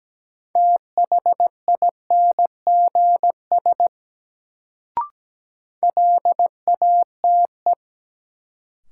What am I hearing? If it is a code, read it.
Morse code, heard as a keyed tone, about 17 wpm, 700 Hz. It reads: THINGS LATE
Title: Sets of 3 Words - New Word Drill - 17wpm